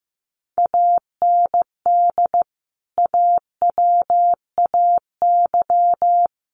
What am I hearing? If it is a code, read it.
Morse code, heard as a keyed tone, about 15 wpm, 700 Hz. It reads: AND AWAY
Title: Sentences from Top 600 Words 15wpm